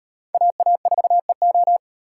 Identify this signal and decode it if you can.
Morse code, heard as a keyed tone, about 38 wpm, 700 Hz. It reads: AA4EO